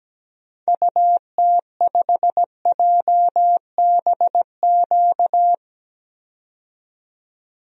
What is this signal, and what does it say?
Morse code, heard as a keyed tone, about 17 wpm, 700 Hz. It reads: UT5JBQ
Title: DXCC-100 Call Signs - 6 Characters - 17wpm